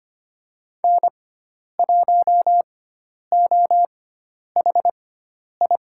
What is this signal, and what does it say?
Morse code, heard as a keyed tone, about 25 wpm, spaced as for 10 wpm, 700 Hz. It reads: N1OHI
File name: Lesson 16g - Letter c - Call Signs 25wpm